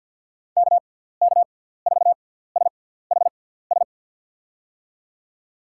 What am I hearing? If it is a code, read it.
Morse code, heard as a keyed tone, about 47 wpm, spaced as for 17 wpm, 700 Hz. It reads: KK4SHS
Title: USA Call Signs - 6 Characters - 6x Character Spacing - 50wpm